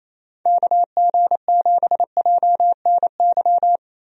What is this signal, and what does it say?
Morse code, heard as a keyed tone, about 28 wpm, 700 Hz. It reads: KG7JNY
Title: USA Call Signs - 6 Characters - Repeated 2x - 28wpm